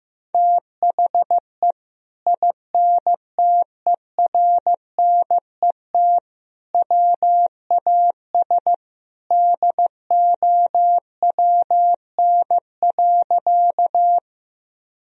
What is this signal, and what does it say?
Morse code, heard as a keyed tone, about 15 wpm, 700 Hz. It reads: THE INTERNET WAS DOWN.
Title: Sentences from Top 1000 Words - Review Part 4 - 15wpm